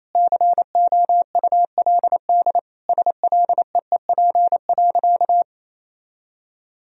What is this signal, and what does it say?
Morse code, heard as a keyed tone, about 28 wpm, 700 Hz. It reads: COULD SLEEP.
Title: Sentences from Top 500 Words and Abbreviations 28wpm